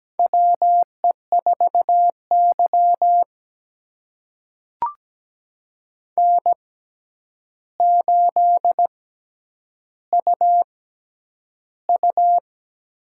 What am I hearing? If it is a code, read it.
Morse code, heard as a keyed tone, about 17 wpm, spaced as for 9 wpm, 700 Hz. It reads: WE4Y N8UU
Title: USA Call Signs - 4 Characters - 6x Character Spacing - 17wpm